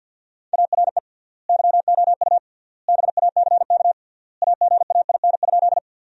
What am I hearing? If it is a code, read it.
Morse code, heard as a keyed tone, about 50 wpm, 700 Hz. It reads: ARE YOU BACK AGAIN?
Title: 3x Word-Spacing - Sentences from Top 200 Words - 50wpm